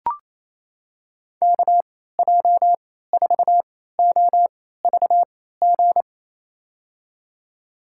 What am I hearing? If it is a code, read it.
Morse code, heard as a keyed tone, about 28 wpm, spaced as for 16 wpm, 700 Hz. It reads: KJ4OVG
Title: USA Call Signs - 6 Characters - 3x Character Spacing - 28wpm